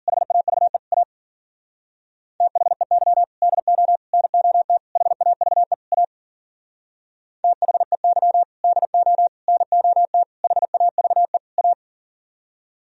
Morse code, heard as a keyed tone, about 52 wpm, 700 Hz. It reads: HAVE A THEY DO NOT HAVE A THEY DO NOT HAVE A